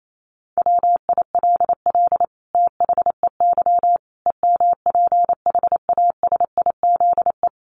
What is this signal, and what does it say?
Morse code, heard as a keyed tone, about 28 wpm, 700 Hz. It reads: WILL THEY EMPHASIZE
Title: Sentences from Top 1500 Words 28wpm